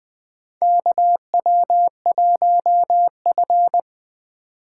Morse code, heard as a keyed tone, about 20 wpm, 700 Hz. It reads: KW1F
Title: USA Call Signs - 4 Characters - 3x Character Spacing - 20wpm